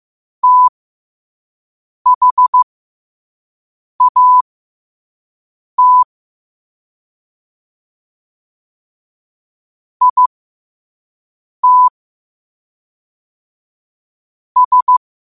Morse code, heard as a keyed tone, about 15 wpm, spaced as for 5 wpm, 1000 Hz. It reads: THAT IT S
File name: qotdfiles / 5wpm